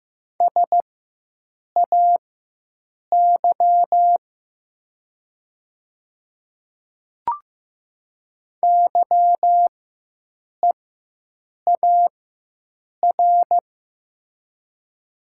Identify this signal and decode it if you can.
Morse code, heard as a keyed tone, about 15 wpm, spaced as for 7 wpm, 700 Hz. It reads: SAY YEAR
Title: 4x Character Spacing - Top 200 Words - 15wpm